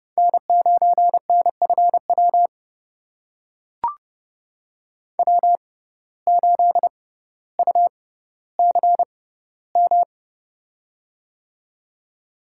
Morse code, heard as a keyed tone, about 30 wpm, spaced as for 15 wpm, 700 Hz. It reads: N9NFW W8UCM
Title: USA Call Signs - 6x Character Spacing - 30wpm